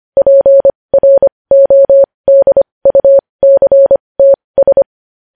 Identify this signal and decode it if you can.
Morse code, heard as a keyed tone, about 25 wpm, spaced as for 20 wpm, 550 Hz. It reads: PRODUCTS
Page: news headlines in morse code at 20 wpm